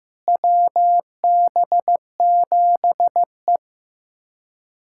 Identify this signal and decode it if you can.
Morse code, heard as a keyed tone, about 16 wpm, 700 Hz. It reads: WB7E
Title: USA Callsign DE Callsign - 4 Characters - 15wpm